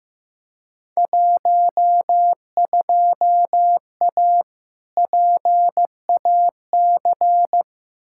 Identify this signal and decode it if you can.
Morse code, heard as a keyed tone, about 15 wpm, 700 Hz. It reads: 12A PAC
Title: Field Day Exchanges Encore - 15wpm